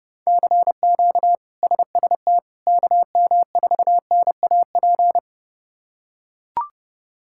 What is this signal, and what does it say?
Morse code, heard as a keyed tone, about 30 wpm, 700 Hz. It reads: CQ SST KM4NAP
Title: K1USN SST QSO Practice - 30wpm